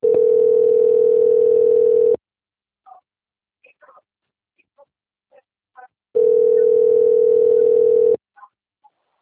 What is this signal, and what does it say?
Ringback tone (North America), 440 + 480 Hz.